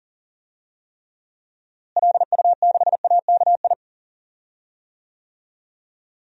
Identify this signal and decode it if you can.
Morse code, heard as a keyed tone, about 40 wpm, 700 Hz. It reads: LU6AKI
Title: DXCC-100 Call Signs - 6 Characters - Repeated 3x - 40wpm